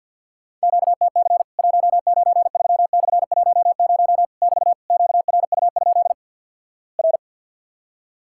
Morse code, heard as a keyed tone, about 48 wpm, 700 Hz. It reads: QTC 193/10 <BT> QRU? R
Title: Worked All Europe DX Contest - QTC Exchange - 50wpm